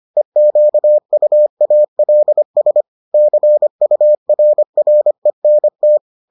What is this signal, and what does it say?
Morse code, heard as a keyed tone, about 25 wpm, 600 Hz. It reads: EQUALS CURRENT